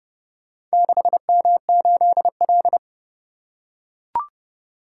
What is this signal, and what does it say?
Morse code, heard as a keyed tone, about 30 wpm, 700 Hz. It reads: 6M8L